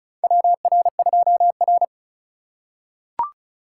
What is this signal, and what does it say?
Morse code, heard as a keyed tone, about 36 wpm, 700 Hz. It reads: WR2R
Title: USA Call Signs - 4 Characters - 3x Character Spacing - 35wpm